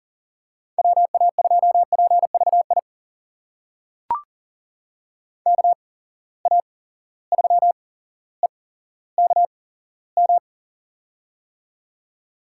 Morse code, heard as a keyed tone, about 37 wpm, spaced as for 17 wpm, 700 Hz. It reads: WA2PVI KA3EKM
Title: USA Call Signs - 6 Characters - 8x Character Spacing - 40wpm